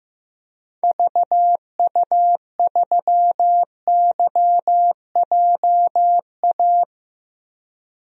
Morse code, heard as a keyed tone, about 15 wpm, 700 Hz. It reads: VU3YJA
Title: DXCC-100 Call Signs - 6 Characters - Repeated 2x - 15wpm